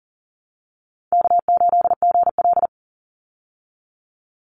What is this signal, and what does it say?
Morse code, heard as a keyed tone, about 39 wpm, 700 Hz. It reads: K8GL
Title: Easy Call Signs Repeated 2x - 40wpm